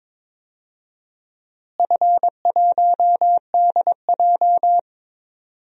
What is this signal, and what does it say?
Morse code, heard as a keyed tone, about 22 wpm, 700 Hz. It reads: F1DJ
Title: DXCC-100 Call Signs - 4 Characters - Repeated 3x - 22wpm